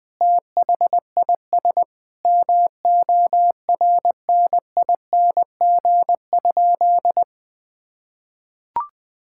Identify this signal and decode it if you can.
Morse code, heard as a keyed tone, about 21 wpm, 700 Hz. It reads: THIS MORNING?
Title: Sentences from Top 1600 Words 20wpm